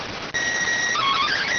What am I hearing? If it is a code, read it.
SSTV (slow-scan television), PD 290.